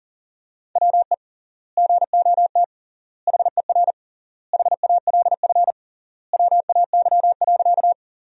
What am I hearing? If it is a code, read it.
Morse code, heard as a keyed tone, about 40 wpm, 700 Hz. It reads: WE GOT HER HALF WAY.
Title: 3x Word-Spacing - Sentences from Top 400 Words - 40wpm